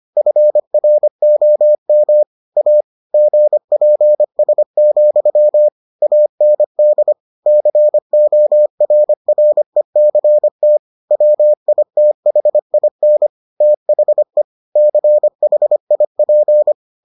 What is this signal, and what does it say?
Morse code, heard as a keyed tone, about 25 wpm, 600 Hz. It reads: FROM A GPS, AND CORRECT WITHIN THE CHIP